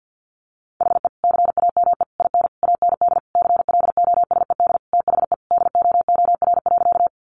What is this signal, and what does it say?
Morse code, heard as a keyed tone, about 48 wpm, 700 Hz. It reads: HE CAME IN AND CLOSED THE DOOR.
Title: Sentences from Top 1400 Words 50wpm